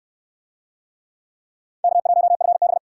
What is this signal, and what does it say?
Morse code, heard as a keyed tone, about 67 wpm, 700 Hz. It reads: K1FB